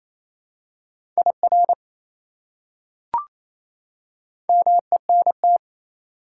Morse code, heard as a keyed tone, about 28 wpm, 700 Hz. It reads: IR MENT
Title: Prefixes and Suffixes 28wpm